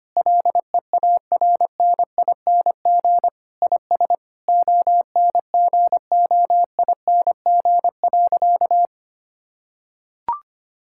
Morse code, heard as a keyed tone, about 25 wpm, 700 Hz. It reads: LEARNING IS ONGOING.
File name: practice / 25wpm / Sentences with 3-Word Phrases - 25wpm